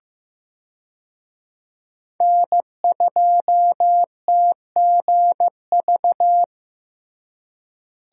Morse code, heard as a keyed tone, about 15 wpm, 700 Hz. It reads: N2TGV
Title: USA Call Signs - 5 Characters - Repeated 3x - 15wpm